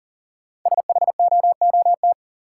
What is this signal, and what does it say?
Morse code, heard as a keyed tone, about 40 wpm, 700 Hz. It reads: SHOOT
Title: Sentences from Top 1200 Words - New Word Drill - 40wpm